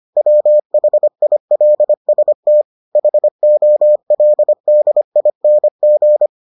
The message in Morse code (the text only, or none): WHILST HOLDING